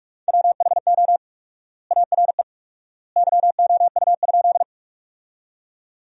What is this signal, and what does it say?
Morse code, heard as a keyed tone, about 44 wpm, 700 Hz. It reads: WHO ARE YOU?
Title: 4x Word-Spacing - Sentences from Top 100 Words - 45wpm